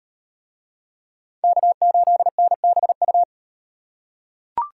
Morse code, heard as a keyed tone, about 38 wpm, 700 Hz. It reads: K8NBU